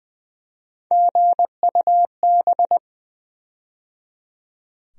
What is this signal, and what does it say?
Morse code, heard as a keyed tone, about 20 wpm, 700 Hz. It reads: GUB
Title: USA Call Sign Postfixes - 3 Letters 20wpm